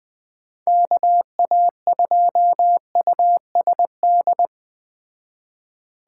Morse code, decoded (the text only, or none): KA2USD